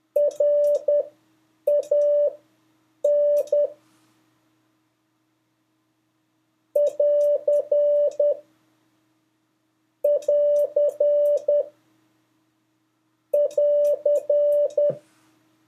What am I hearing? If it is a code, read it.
Morse code, heard as a keyed tone, about 10 wpm, spaced as for 7 wpm, 580 Hz. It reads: RAN <AR> <AR> <AR>
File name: 3of5-words-2